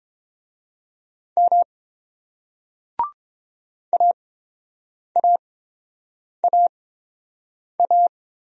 Morse code, heard as a keyed tone, about 30 wpm, spaced as for 7 wpm, 700 Hz. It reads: M AAAA